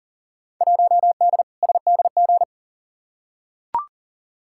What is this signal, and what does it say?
Morse code, heard as a keyed tone, about 40 wpm, 700 Hz. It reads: 1D SDG
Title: Field Day Exchanges - 40wpm